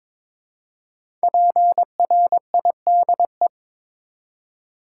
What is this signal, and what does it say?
Morse code, heard as a keyed tone, about 22 wpm, 700 Hz. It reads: PRIDE
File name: 5 Letter Words - First 3 Letters Given - 22wpm